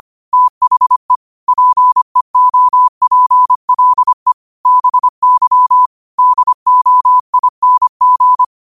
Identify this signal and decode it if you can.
Morse code, heard as a keyed tone, about 25 wpm, 1000 Hz. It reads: THE PEOPLE BY DOING